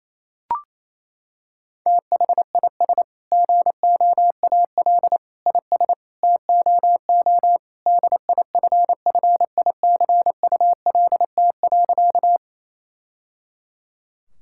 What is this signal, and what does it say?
Morse code, heard as a keyed tone, about 28 wpm, 700 Hz. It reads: THIS GOAL IS TOO DIFFICULT.
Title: Sentences from Top 1000 Words - Review Part 5 - 28wpm